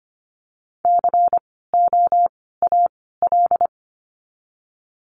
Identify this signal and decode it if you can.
Morse code, heard as a keyed tone, about 25 wpm, 700 Hz. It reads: C O A L
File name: course / 25:10wpm / 094 Lesson 16f - Letter c - Words 25wpm